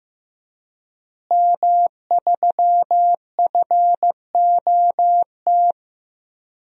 Morse code, heard as a keyed tone, about 15 wpm, 700 Hz. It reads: M3FOT
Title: DXCC-100 Call Signs - 5 Characters - Repeated 3x - 15wpm